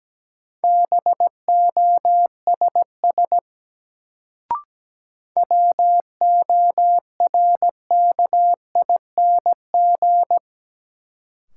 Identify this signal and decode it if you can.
Morse code, heard as a keyed tone, about 17 wpm, 700 Hz. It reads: BOSS WORKING
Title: Sentences from Top 700 Words - New Word Drill - 17wpm